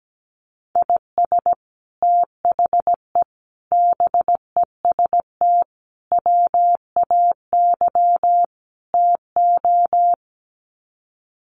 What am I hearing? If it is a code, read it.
Morse code, heard as a keyed tone, about 17 wpm, 700 Hz. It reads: IS THE BEST WAY TO
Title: Sets of 5 Words Encore 17wpm